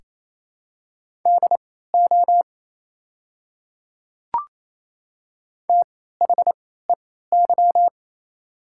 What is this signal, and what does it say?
Morse code, heard as a keyed tone, about 27 wpm, spaced as for 16 wpm, 700 Hz. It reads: DO THEY
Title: 3x Character Spacing - Top 100 Words - 28wpm